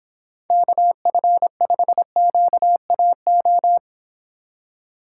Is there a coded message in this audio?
Yes: Morse code from 0.5 to 3.8 s, heard as a keyed tone, about 25 wpm, 700 Hz. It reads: KF5QAO